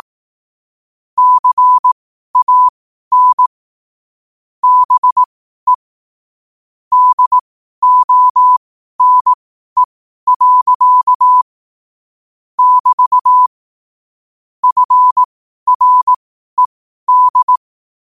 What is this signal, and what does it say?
Morse code, heard as a keyed tone, about 18 wpm, spaced as for 12 wpm, 1000 Hz. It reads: CAN BE DONE. <BT> FRED
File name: qotdfiles / 12wpm